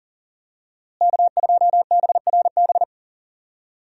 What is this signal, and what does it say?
Morse code, heard as a keyed tone, about 40 wpm, 700 Hz. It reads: K2BRB
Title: USA Call Signs - 5 Characters - Repeated 2x - 40wpm